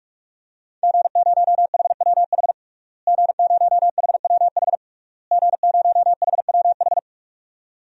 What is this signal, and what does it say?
Morse code, heard as a keyed tone, about 46 wpm, 700 Hz. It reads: G0HWH G0HWH G0HWH
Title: DXCC-100 Call Signs - 5 Characters - Repeated 3x - 45wpm